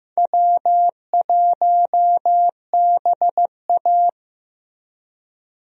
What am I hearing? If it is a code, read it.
Morse code, heard as a keyed tone, about 15 wpm, 700 Hz. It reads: W1BA
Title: USA Call Signs - 4 Characters - Repeated 2x - 15wpm